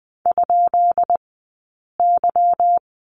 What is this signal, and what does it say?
Morse code, heard as a keyed tone, about 20 wpm, 700 Hz. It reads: ? Y
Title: 158 Lesson 26d - Letter g - Two Characters 20wpm